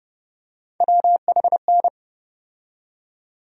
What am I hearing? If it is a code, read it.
Morse code, heard as a keyed tone, about 31 wpm, 700 Hz. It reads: WHN
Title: USA Call Sign Postfixes - 3 Letters 30wpm